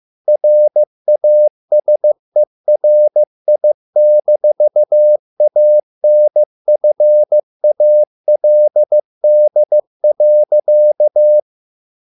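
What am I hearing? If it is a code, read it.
Morse code, heard as a keyed tone, about 15 wpm, 600 Hz. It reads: RASERI-ANFALD.
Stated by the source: Morse code at a slow 15 wpm from CW med Gnister.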